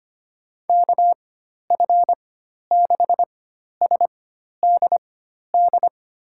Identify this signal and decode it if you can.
Morse code, heard as a keyed tone, about 25 wpm, spaced as for 12 wpm, 700 Hz. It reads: KF6SDD
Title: USA Call Signs - 6 Characters - 4x Character Spacing - 25wpm